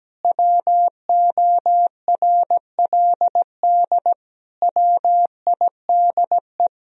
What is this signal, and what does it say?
Morse code, heard as a keyed tone, about 17 wpm, 700 Hz. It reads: WORLD WIDE
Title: Sentences from Top 1000 Words - Review Part 4 - 17wpm